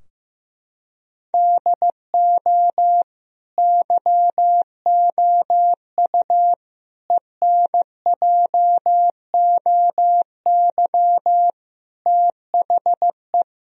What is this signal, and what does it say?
Morse code, heard as a keyed tone, about 16 wpm, 700 Hz. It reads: DO YOU ENJOY THE
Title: Sentences from Top 1000 Words - Review Part 4 - 15wpm